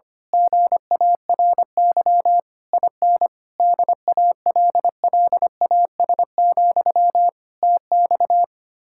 Morse code, heard as a keyed tone, about 24 wpm, 700 Hz. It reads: GARY IN DALLAS, TX
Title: Speed-Racing Common QSO Phrases - 25wpm